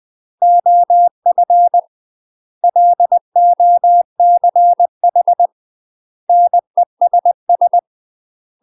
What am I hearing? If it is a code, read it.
Morse code, heard as a keyed tone, about 20 wpm, 700 Hz. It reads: OF LOCH NESS
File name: UK MASTER QSOs TRAINING / 20WPM / UK MASTER QSOs 0074